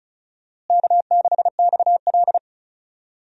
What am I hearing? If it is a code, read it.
Morse code, heard as a keyed tone, about 36 wpm, 700 Hz. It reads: K6XL
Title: USA Call Signs - 4 Characters - Repeated 2x - 35wpm